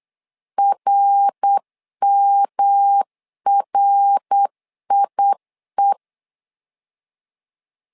Morse code, heard as a keyed tone, about 8 wpm, 800 Hz. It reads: RMRIE
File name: LEZIONI / 3 MWRO / 3 MWR 8wpm 2444